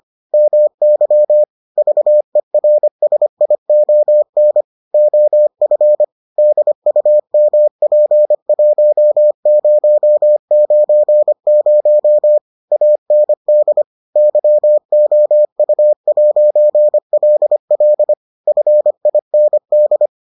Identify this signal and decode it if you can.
Morse code, heard as a keyed tone, about 25 wpm, 600 Hz. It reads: MY VERSION OF DUMP1090 AND YOU'LL FIND